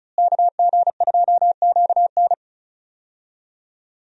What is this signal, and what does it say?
Morse code, heard as a keyed tone, about 36 wpm, 700 Hz. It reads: KG2QN
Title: USA Call Signs - 5 Characters - Repeated 2x - 35wpm